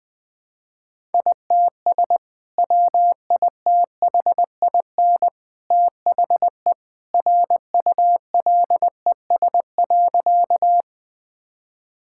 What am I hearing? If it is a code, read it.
Morse code, heard as a keyed tone, about 20 wpm, 700 Hz. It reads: ITS WITHIN THE RULES.
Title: Sentences with 2-Word Phrases - Spoken Beginning - 20wpm